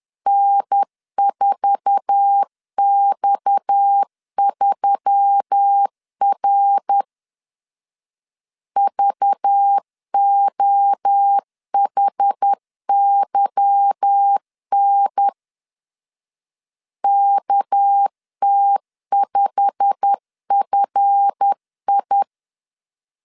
Morse code, heard as a keyed tone, about 10 wpm, 800 Hz. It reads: N4X3R VOHYN KT5FI
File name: LEZIONI / 8 YJ45 / 8 YJ45 10wpm 2549